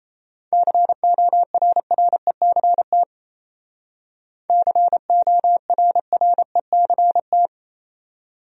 Morse code, heard as a keyed tone, about 34 wpm, 700 Hz. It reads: CORRECT CORRECT
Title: Speed-Racing Top 500 Words 28wpm